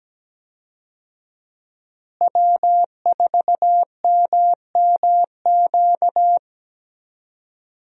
Morse code, heard as a keyed tone, about 17 wpm, 700 Hz. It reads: W4MMQ
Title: USA Call Signs - 5 Characters - Repeated 3x - 17wpm